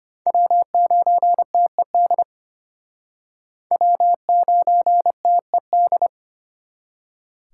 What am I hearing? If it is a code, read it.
Morse code, heard as a keyed tone, about 32 wpm, 700 Hz. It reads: W9TED W9TED